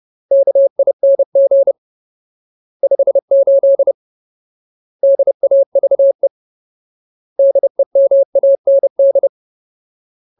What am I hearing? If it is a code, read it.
Morse code, heard as a keyed tone, about 30 wpm, 550 Hz. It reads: KING 58 DAVE DEMAND
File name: Ultimate Word List / 30WPM Ultimate